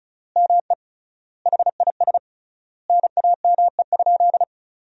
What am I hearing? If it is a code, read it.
Morse code, heard as a keyed tone, about 34 wpm, 700 Hz. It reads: ME HIS NAME?
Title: 3x Word-Spacing - Sentences from Top 400 Words - 35wpm